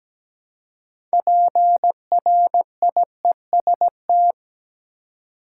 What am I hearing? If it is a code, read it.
Morse code, heard as a keyed tone, about 17 wpm, 700 Hz. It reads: PRIEST
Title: Six Letter Words - 17wpm